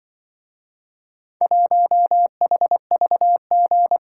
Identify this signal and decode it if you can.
Morse code, heard as a keyed tone, about 24 wpm, 700 Hz. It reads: 1HVG
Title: Speed-Racing - Four Letter-Number Combinations - 20wpm